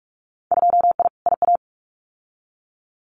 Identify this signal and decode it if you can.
Morse code, heard as a keyed tone, about 46 wpm, 700 Hz. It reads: 2I IA
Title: Winter Field Day Exchanges Encore 45wpm